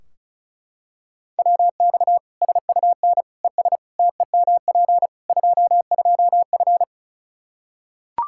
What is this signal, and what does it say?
Morse code, heard as a keyed tone, about 36 wpm, 700 Hz. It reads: WX SUN ES TEMP 22F
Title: Practice QSO - Exchange 2 - Info - 35wpm